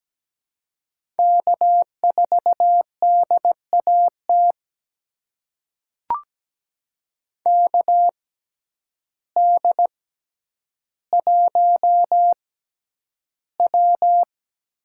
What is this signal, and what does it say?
Morse code, heard as a keyed tone, about 17 wpm, spaced as for 9 wpm, 700 Hz. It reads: K4DAT KD1W
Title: USA Call Signs - 5 Characters - 6x Character Spacing - 17wpm